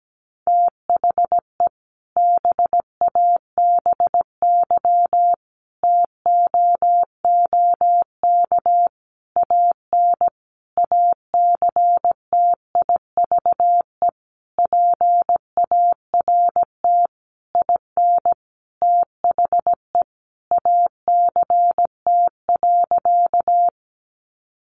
Morse code, heard as a keyed tone, about 17 wpm, 700 Hz. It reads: THE BABY TOOK AN ACTIVE PART IN THE ACT.
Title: Sentences from Top 1300 Words 17wpm